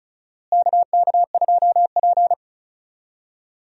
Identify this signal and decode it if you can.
Morse code, heard as a keyed tone, about 36 wpm, 700 Hz. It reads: KK2P
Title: USA Call Signs - 4 Characters - Repeated 3x - 35wpm